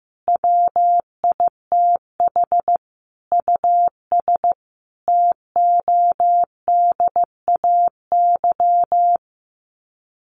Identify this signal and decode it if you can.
Morse code, heard as a keyed tone, about 15 wpm, 700 Hz. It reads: WITH US TODAY
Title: Sets of 5 Words Encore2 15wpm